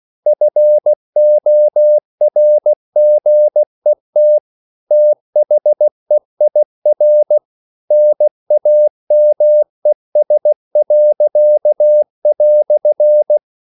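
Morse code, heard as a keyed tone, about 16 wpm, 600 Hz. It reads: FORGET THEIR NAMES."
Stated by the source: English quotations sent in CW as practice at 16 words per minute.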